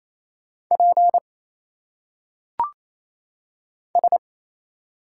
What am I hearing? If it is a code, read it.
Morse code, heard as a keyed tone, about 27 wpm, 700 Hz. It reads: P S
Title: Single Letters - Phonetic Pronunciation 28wpm